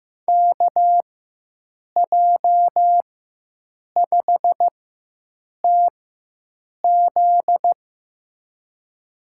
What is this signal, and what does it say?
Morse code, heard as a keyed tone, about 15 wpm, spaced as for 7 wpm, 700 Hz. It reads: KJ5TZ
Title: USA Call Signs - 5 Characters - 4x Character Spacing - 15wpm